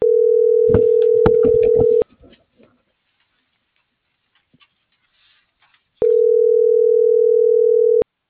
Ringback tone (North America), 440 + 480 Hz.